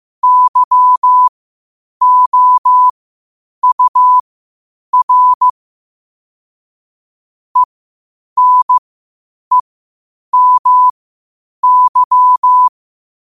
Morse code, heard as a keyed tone, about 15 wpm, spaced as for 8 wpm, 1000 Hz. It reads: YOUR ENEMY
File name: qotdfiles / 8wpm